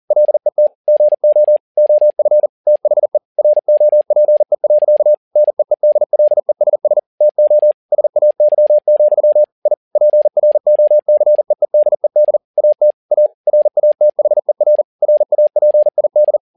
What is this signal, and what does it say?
Morse code, heard as a keyed tone, about 40 wpm, 600 Hz. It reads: LET GO OF THE ROPE. NEEDLESS TO SAY, I PROCEEDED AT A RATHER RAPID
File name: learning center / tower trouble / tower touble 40wpm